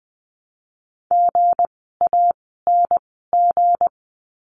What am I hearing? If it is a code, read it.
Morse code, heard as a keyed tone, about 20 wpm, 700 Hz. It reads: G A N G
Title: 160 Lesson 26f - Letter g - Words 20wpm